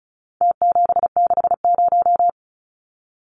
Morse code, heard as a keyed tone, about 34 wpm, 700 Hz. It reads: T760
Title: Numbers - Contest Style - 4-Digit 35wpm